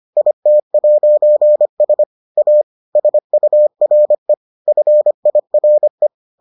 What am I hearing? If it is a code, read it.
Morse code, heard as a keyed tone, about 25 wpm, 600 Hz. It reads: IT'S A SURE FIRE